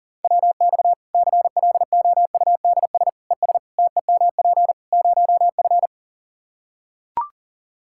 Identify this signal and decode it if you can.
Morse code, heard as a keyed tone, about 40 wpm, 700 Hz. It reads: WX CLOUDS ES TEMP 0F
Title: Practice QSO - Exchange 2 - Info - 40wpm